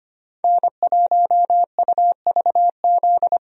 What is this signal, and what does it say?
Morse code, heard as a keyed tone, about 25 wpm, 700 Hz. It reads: N1UVZ
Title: USA Call Signs - 5 Characters - 25wpm